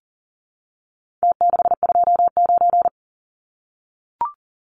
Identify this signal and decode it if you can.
Morse code, heard as a keyed tone, about 40 wpm, 700 Hz. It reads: T629
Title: Numbers - Contest Style - 4-Digit 40wpm